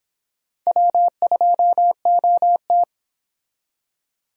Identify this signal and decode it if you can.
Morse code, heard as a keyed tone, about 26 wpm, 700 Hz. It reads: W2OT